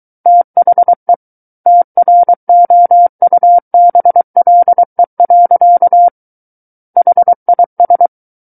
Morse code, heard as a keyed tone, about 23 wpm, 700 Hz. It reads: THE TROUBLE. HIS
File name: Book One / 14-Chapter 14 - 23 WPM